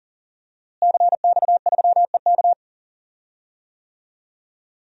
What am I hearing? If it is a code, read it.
Morse code, heard as a keyed tone, about 40 wpm, 700 Hz. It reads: CX3EK
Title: DXCC-100 Call Signs - 5 Characters - Repeated 2x - 40wpm